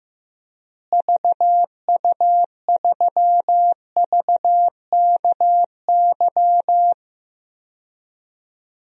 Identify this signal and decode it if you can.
Morse code, heard as a keyed tone, about 15 wpm, 700 Hz. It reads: VU3VKY
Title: DXCC-100 Call Signs - 6 Characters - Repeated 3x - 15wpm